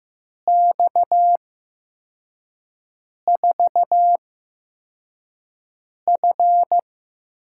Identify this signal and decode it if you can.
Morse code, heard as a keyed tone, about 15 wpm, spaced as for 4 wpm, 700 Hz. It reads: X4F
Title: USA Call Signs - 5 Characters - 8x Character Spacing - 15wpm